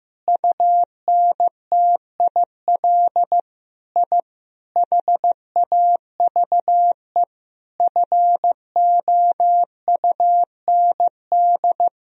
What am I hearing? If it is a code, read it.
Morse code, heard as a keyed tone, about 15 wpm, 700 Hz. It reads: UNTIL I HAVE FOUND
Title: Sentences from Top 1000 Words - Review Part 1 - 15wpm